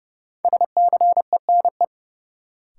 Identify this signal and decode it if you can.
Morse code, heard as a keyed tone, about 31 wpm, 700 Hz. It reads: SCENE
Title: Sentences from Top 1100 Words - New Word Drill Alt - 30wpm